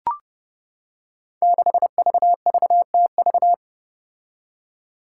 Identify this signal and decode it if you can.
Morse code, heard as a keyed tone, about 32 wpm, 700 Hz. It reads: 6VVTV
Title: Five Challenging Letter-Number Combinations 30wpm